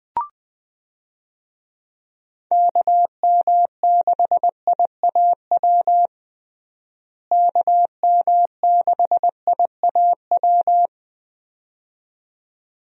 Morse code, heard as a keyed tone, about 20 wpm, 700 Hz. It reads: KM6IAW KM6IAW
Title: USA Call Signs - 6 Characters - Repeated 2x - 20wpm